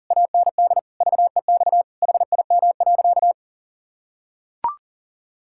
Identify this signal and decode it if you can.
Morse code, heard as a keyed tone, about 40 wpm, 700 Hz. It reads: AND VEX HIM.
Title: Pangram Challenge v2 - 40wpm